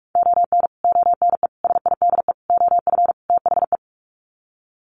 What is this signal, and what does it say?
Morse code, heard as a keyed tone, about 44 wpm, 700 Hz. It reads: ON ONE SIDE OF THE
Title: Sets of 5 Words 45wpm